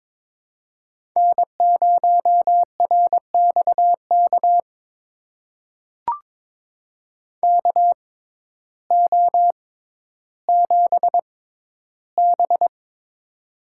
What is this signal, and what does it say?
Morse code, heard as a keyed tone, about 22 wpm, spaced as for 12 wpm, 700 Hz. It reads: N0RXK KO7B
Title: USA Call Signs - 6x Character Spacing - 22wpm